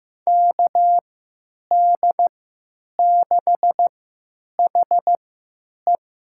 Morse code, heard as a keyed tone, about 15 wpm, spaced as for 9 wpm, 700 Hz. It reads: KD6HE
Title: USA Call Signs - 3x Character Spacing - 15wpm